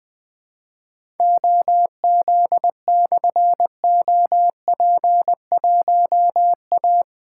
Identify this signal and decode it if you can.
Morse code, heard as a keyed tone, about 20 wpm, 700 Hz. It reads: OZ/OP1A
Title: Difficult Call Signs 20wpm